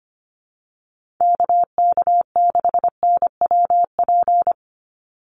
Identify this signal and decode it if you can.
Morse code, heard as a keyed tone, about 25 wpm, 700 Hz. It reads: KK6NWP
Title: USA Call Signs 25wpm